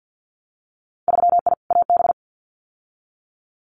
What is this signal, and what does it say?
Morse code, heard as a keyed tone, about 50 wpm, 700 Hz. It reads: 3I AB
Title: Winter Field Day Exchanges Encore 50wpm